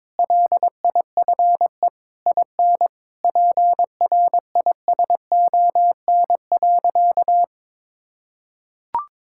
Morse code, heard as a keyed tone, about 22 wpm, 700 Hz. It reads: LIFE IN PRISON.
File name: practice / 22wpm / Sentences with 8-Word Phrases - 22wpm